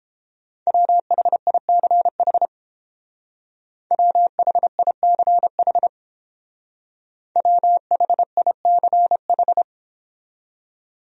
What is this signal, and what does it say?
Morse code, heard as a keyed tone, about 31 wpm, 700 Hz. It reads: WHICH WHICH WHICH